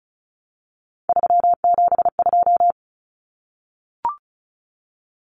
Morse code, heard as a keyed tone, about 34 wpm, 700 Hz. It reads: 372
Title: Common Cut Numbers - 3-Digit 35wpm